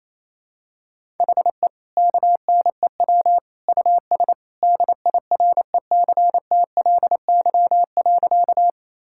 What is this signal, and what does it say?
Morse code, heard as a keyed tone, about 28 wpm, 700 Hz. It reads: HE KNEW US DIRECTLY.
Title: Sentences from Top 1000 Words - Review Part 5 - 28wpm